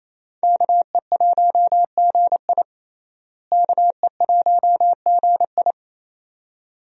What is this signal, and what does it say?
Morse code, heard as a keyed tone, about 28 wpm, 700 Hz. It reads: KE1GI KE1GI
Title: USA Call Signs - 5 Characters - Repeated 2x - 28wpm